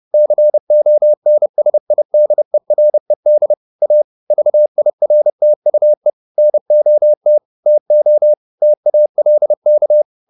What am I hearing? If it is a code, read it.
Morse code, heard as a keyed tone, about 30 wpm, 600 Hz. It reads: CONSIDERED A VIRTUE NOT TO TALK
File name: the old man and the sea / the old man and the sea 1 30wpm